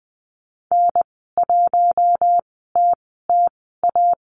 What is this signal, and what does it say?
Morse code, heard as a keyed tone, about 20 wpm, spaced as for 15 wpm, 700 Hz. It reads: N1TTA
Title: 048 Lesson 9g - Number 4 - Call Signs 20wpm